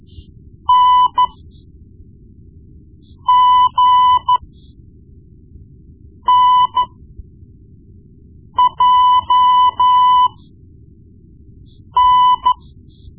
Morse code, heard as a keyed tone, about 13 wpm, spaced as for 4 wpm, 1000 Hz. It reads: NGNJN
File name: Morse Lessons / Morse Lessons / Tape-1B